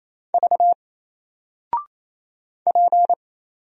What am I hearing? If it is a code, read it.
Morse code, heard as a keyed tone, about 27 wpm, 700 Hz. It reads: V P
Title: Single Letter-Number Alt 28wpm